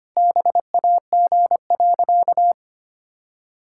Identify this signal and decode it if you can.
Morse code, heard as a keyed tone, about 25 wpm, 700 Hz. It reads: BAG.
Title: Pangram Challenge v2 - 25wpm